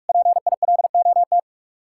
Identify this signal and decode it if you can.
Morse code, heard as a keyed tone, about 55 wpm, 700 Hz. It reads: PILOT
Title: Speed-Racing Common QSO Phrases Encore - 45wpm